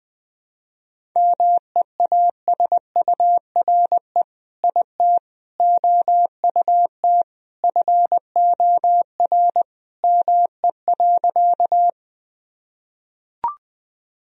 Morse code, heard as a keyed tone, about 20 wpm, 700 Hz. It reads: MEASURE IT OUT FOR ME.
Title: Sentences from Top 1000 Words - Review Part 2 - 20wpm